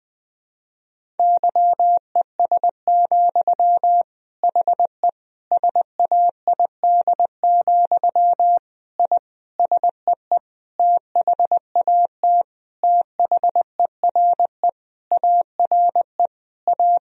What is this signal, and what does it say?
Morse code, heard as a keyed tone, about 21 wpm, 700 Hz. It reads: YES, HE SAID, I SEE THAT THERE ARE A
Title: Sentences from Top 1000 Words - Review Part 1 - 20wpm